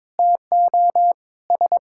Morse code, heard as a keyed tone, about 22 wpm, 700 Hz. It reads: TO S
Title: Sentences from Top 1000 Words - Review Part 5 - 22wpm